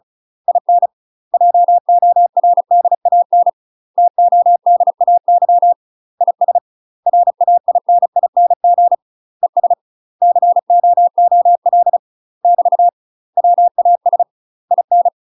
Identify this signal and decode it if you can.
Morse code, heard as a keyed tone, about 36 wpm, 700 Hz. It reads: IN JORDAN TODAY IS RAINING ES COOL <BT> WAS IN
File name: UK MASTER QSOs TRAINING / 35WPM / UK MASTER QSOs 0022